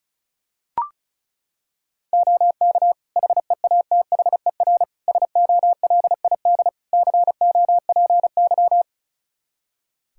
Morse code, heard as a keyed tone, about 36 wpm, 700 Hz. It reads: OK HEATHER SOLID COPY
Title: K1USN SST QSO Practice 35wpm